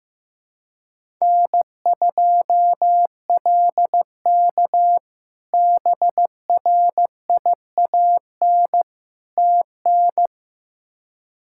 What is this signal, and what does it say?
Morse code, heard as a keyed tone, about 15 wpm, 700 Hz. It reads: N2LK BRIAN TN
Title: K1USN SST QSO Practice - 15wpm